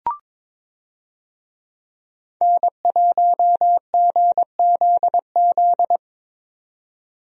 Morse code, heard as a keyed tone, about 22 wpm, 700 Hz. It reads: N1GZZ
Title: USA Call Signs - 5 Characters - 22wpm